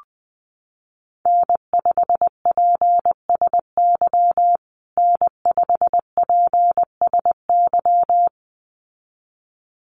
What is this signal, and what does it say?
Morse code, heard as a keyed tone, about 20 wpm, 700 Hz. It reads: N5PSY N5PSY
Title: USA Call Signs Repeated 2x 20wpm